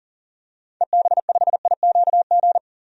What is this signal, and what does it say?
Morse code, heard as a keyed tone, about 40 wpm, 700 Hz. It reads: EB5IQG
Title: DXCC-100 Call Signs - 6 Characters - 40wpm